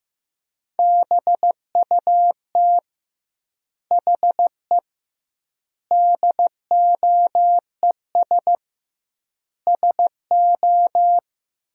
Morse code, heard as a keyed tone, about 15 wpm, 700 Hz. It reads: BUT HE DOES SO
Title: 2x Word-Spacing - Sentences from Top 200 Words - 15wpm